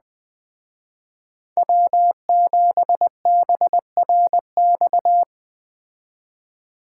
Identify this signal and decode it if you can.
Morse code, heard as a keyed tone, about 20 wpm, 700 Hz. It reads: W7BRX
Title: Speed Racing - USA 1x3 Call Signs 17wpm